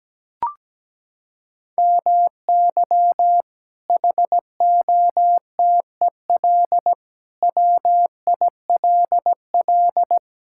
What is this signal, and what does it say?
Morse code, heard as a keyed tone, about 17 wpm, 700 Hz. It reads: MY HOTEL WILL
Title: Sentences from Top 1600 Words 17wpm